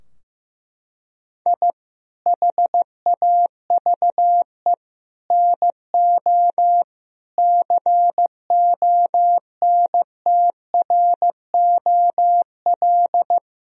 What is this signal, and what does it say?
Morse code, heard as a keyed tone, about 15 wpm, 700 Hz. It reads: I HAVE NO CONTROL
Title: Sentences from Top 1000 Words - Review Part 3 - 15wpm